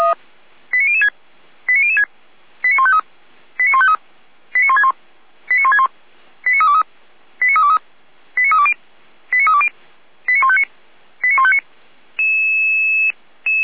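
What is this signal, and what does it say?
Five-tone selective call (ZVEI), heard as quick runs of five tones, tones of about 80 ms. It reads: 89006 89006 89153 89153 89161 89161 89323 89323 89320 89320 89159 89159
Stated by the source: Five-tone ZVEI selective call received by SDR, from feuerwehr-orlamuende.org audio archive (fire brigade Orlamünde).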